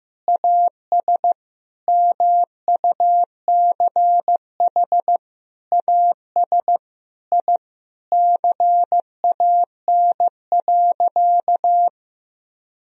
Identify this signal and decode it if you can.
Morse code, heard as a keyed tone, about 15 wpm, 700 Hz. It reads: AS MUCH AS I CAN.
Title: Sentences from Top 1000 Words - Review Part 3 - 15wpm